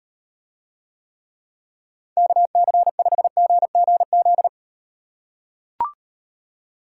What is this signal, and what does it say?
Morse code, heard as a keyed tone, about 38 wpm, 700 Hz. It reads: KC5GGZ